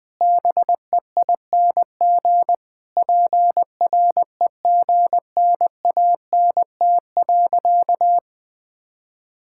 Morse code, heard as a keyed tone, about 20 wpm, 700 Hz. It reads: BEING PREGNANT.
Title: Sentences from Top 1000 Words - Review Part 6 - 20wpm